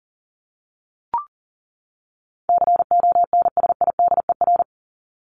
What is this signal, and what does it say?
Morse code, heard as a keyed tone, about 40 wpm, 700 Hz. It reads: CONSIDER
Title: ED Suffix - Top 100 Words with Base Words 40wpm